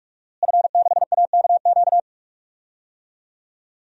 Morse code, heard as a keyed tone, about 46 wpm, 700 Hz. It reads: F6AKQ
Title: DXCC-100 Call Signs - 5 Characters - Repeated 3x - 45wpm